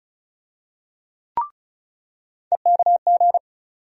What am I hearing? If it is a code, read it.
Morse code, heard as a keyed tone, about 34 wpm, 700 Hz. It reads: EKG
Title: USA Call Sign Postfixes - 3 Letters 35wpm